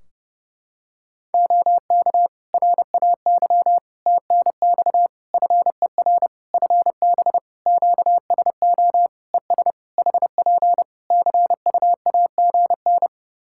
Morse code, heard as a keyed tone, about 30 wpm, 700 Hz. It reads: OK RAY TNX FER FB QSO ES HP CUAGN
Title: Practice QSO - Exchange 4 - Ending - 30wpm